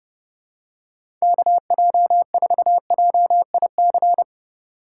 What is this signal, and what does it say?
Morse code, heard as a keyed tone, about 32 wpm, 700 Hz. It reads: KJ4JIC
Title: USA Call Signs - 4x Character Spacing - 30wpm